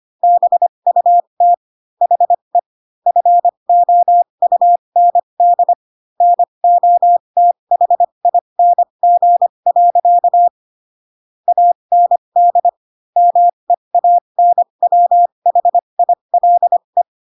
Morse code, heard as a keyed tone, about 25 wpm, spaced as for 22 wpm, 700 Hz. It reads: BUT HE FOUND NOTHING. AND MEANWHILE